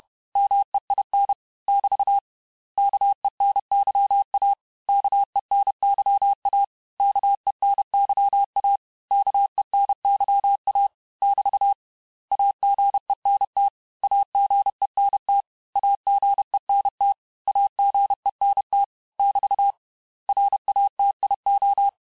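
Morse code, heard as a keyed tone, about 31 wpm, 800 Hz. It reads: MEIN <BT> KENYA KENYA KENYA KENYA <BT> AGENT AGENT AGENT AGENT <BT> RATIO
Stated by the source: Morse lesson of words, each word repeated 4 times, at 29 words per minute.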